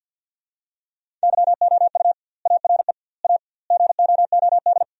Morse code, heard as a keyed tone, about 50 wpm, 700 Hz. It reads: YOU ARE A GOOD
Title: 2x Word-Spacing - Sentences from Top 400 Words - 50wpm